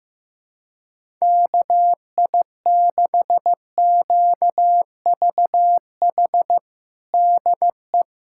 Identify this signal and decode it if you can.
Morse code, heard as a keyed tone, about 15 wpm, 700 Hz. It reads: KI6QVH DE
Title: USA Callsign DE Callsign 15wpm